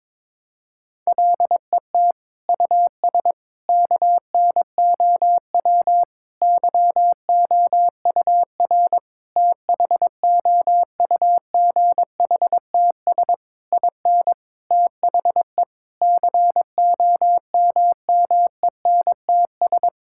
Morse code, heard as a keyed tone, about 22 wpm, 700 Hz. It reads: LET US KNOW YOUR THOUGHTS IN THE COMMENTS
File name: Sets of 9 Words - 22wpm